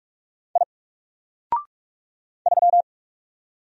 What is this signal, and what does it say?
Morse code, heard as a keyed tone, about 44 wpm, 700 Hz. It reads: I 3
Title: Single Character (Letter:Number:Punctuation:Prosign) 45wpm